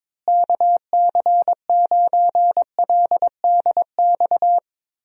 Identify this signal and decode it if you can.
Morse code, heard as a keyed tone, about 22 wpm, 700 Hz. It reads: KC9LDX
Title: USA Call Signs - 6 Characters - Repeated 3x - 22wpm